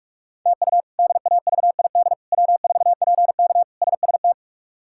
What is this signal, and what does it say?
Morse code, heard as a keyed tone, about 46 wpm, 700 Hz. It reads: TU DAVID W4PK SST
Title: K1USN SST QSO Practice - 45wpm